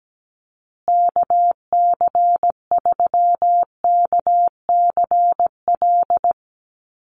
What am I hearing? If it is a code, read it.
Morse code, heard as a keyed tone, about 17 wpm, 700 Hz. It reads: KC3KCL
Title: USA Call Signs Repeated 3x 17wpm